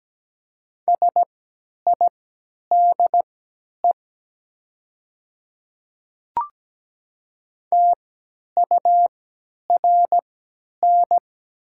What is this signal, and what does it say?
Morse code, heard as a keyed tone, about 17 wpm, spaced as for 10 wpm, 700 Hz. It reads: SIDE TURN
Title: 3x Character Spacing - Top 200 Words - 17wpm